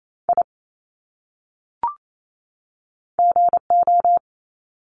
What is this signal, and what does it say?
Morse code, heard as a keyed tone, about 28 wpm, 700 Hz. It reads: I GO
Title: Top 300 Words 28wpm